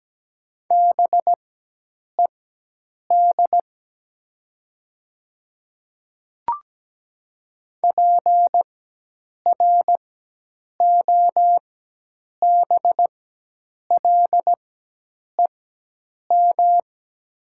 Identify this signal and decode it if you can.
Morse code, heard as a keyed tone, about 17 wpm, spaced as for 8 wpm, 700 Hz. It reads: BED PROBLEM
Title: 4x Character Spacing - Top 500 Words - 17wpm